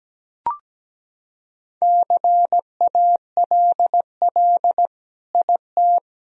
Morse code, heard as a keyed tone, about 17 wpm, 700 Hz. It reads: CALL IT
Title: Sentences from Top 1000 Words - Review Part 4 - 17wpm